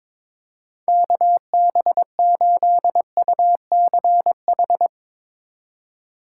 Morse code, heard as a keyed tone, about 22 wpm, 700 Hz. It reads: KB8UCH
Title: USA Call Signs - 6 Characters - 2x Character Spacing - 22wpm